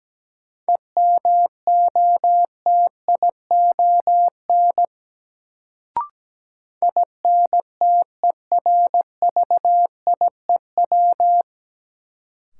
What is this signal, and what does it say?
Morse code, heard as a keyed tone, about 17 wpm, 700 Hz. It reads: EMOTION INTERVIEW
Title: Sentences from Top 1600 Words - New Word Drill Alt - 17wpm